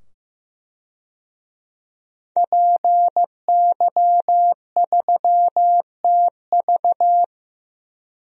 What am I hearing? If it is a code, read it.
Morse code, heard as a keyed tone, about 16 wpm, 700 Hz. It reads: PY3TV
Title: DXCC-100 Call Signs - 5 Characters - 15wpm